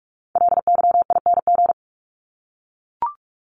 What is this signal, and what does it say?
Morse code, heard as a keyed tone, about 44 wpm, 700 Hz. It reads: LYING
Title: ING Suffix - Top 100 Words with Base Words 45wpm